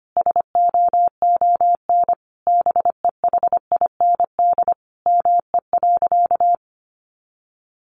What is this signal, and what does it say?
Morse code, heard as a keyed tone, about 25 wpm, 700 Hz. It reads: SOON BEHIND ME.
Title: Sentences from Top 500 Words 25wpm